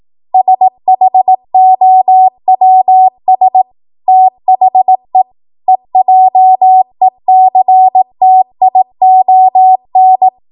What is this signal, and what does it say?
Morse code, heard as a keyed tone, about 18 wpm, 750 Hz. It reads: SHOWS THE EJECTION
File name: Morse / 18WPM